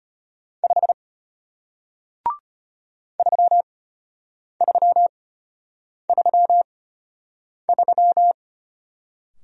Morse code, heard as a keyed tone, about 33 wpm, spaced as for 8 wpm, 700 Hz. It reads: 5 3333